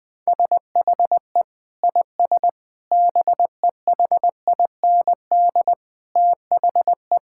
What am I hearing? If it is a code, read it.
Morse code, heard as a keyed tone, about 21 wpm, 700 Hz. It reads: SHE IS BEHIND THE
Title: Sentences from Top 1600 Words 20wpm